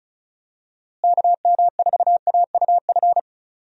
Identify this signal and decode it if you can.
Morse code, heard as a keyed tone, about 36 wpm, 700 Hz. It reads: KM4AUF
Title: USA Call Signs - 6 Characters - Repeated 3x - 35wpm